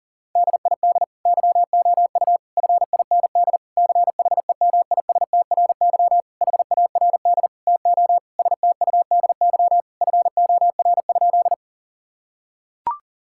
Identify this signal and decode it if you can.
Morse code, heard as a keyed tone, about 40 wpm, 700 Hz. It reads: DID YOU FIND CHEMISTRY HARD TO STUDY FOR?
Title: Sentences from Top 1000 Words - Review Part 6 - 40wpm